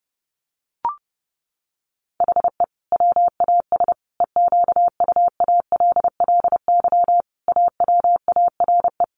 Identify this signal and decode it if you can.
Morse code, heard as a keyed tone, about 30 wpm, 700 Hz. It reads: HE WAS EQUALLY AWARE
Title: Sentences from Top 1400 Words 30wpm